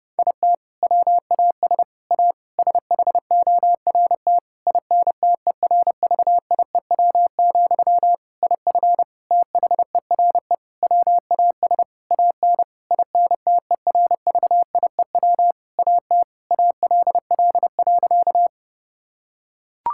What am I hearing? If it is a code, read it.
Morse code, heard as a keyed tone, about 30 wpm, 700 Hz. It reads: IT WAS A SHORT INTERVIEW, IF THERE WAS AN INTERVIEW AT ALL.
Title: Sentences from Top 1600 Words 30wpm